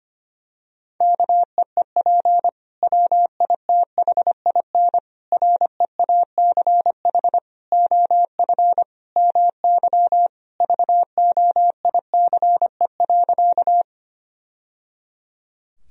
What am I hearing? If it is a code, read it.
Morse code, heard as a keyed tone, about 25 wpm, 700 Hz. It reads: KEEP WITHIN REACH OF MY VOICE.
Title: Sentences from Top 1000 Words - Review Part 4 - 25wpm